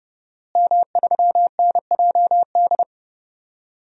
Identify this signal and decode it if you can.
Morse code, heard as a keyed tone, about 30 wpm, 700 Hz. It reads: M3NJD
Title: DXCC-100 Call Signs - 5 Characters - Repeated 2x - 30wpm